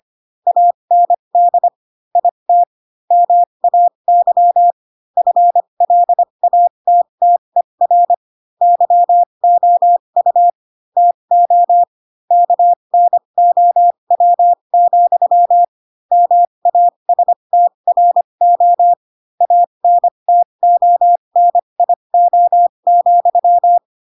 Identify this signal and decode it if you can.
Morse code, heard as a keyed tone, about 25 wpm, spaced as for 22 wpm, 700 Hz. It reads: AND IT MAY FLATTER YOU TO KNOW, MASTRO ANTONIO,
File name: Pinocchio / Pinocchio ch2 18 25 ik1hge